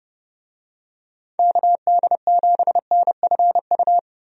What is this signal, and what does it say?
Morse code, heard as a keyed tone, about 30 wpm, 700 Hz. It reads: KD7NFU
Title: USA Call Signs - 6 Characters - Repeated 2x - 30wpm